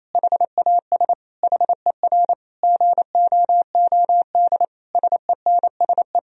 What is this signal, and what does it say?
Morse code, heard as a keyed tone, about 28 wpm, 700 Hz. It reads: HAS HER GOOD SENSE
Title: Sentences from Top 1000 Words - Review Part 4 - 28wpm